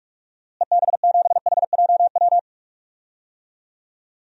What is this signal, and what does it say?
Morse code, heard as a keyed tone, about 42 wpm, 700 Hz. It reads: EB7HJW
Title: DXCC-100 Call Signs - 6 Characters - Repeated 2x - 45wpm